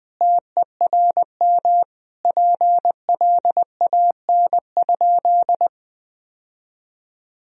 Morse code, heard as a keyed tone, about 19 wpm, 700 Hz. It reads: TERM PLAN?
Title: Sentences from Top 1000 Words - Review Part 4 - 20wpm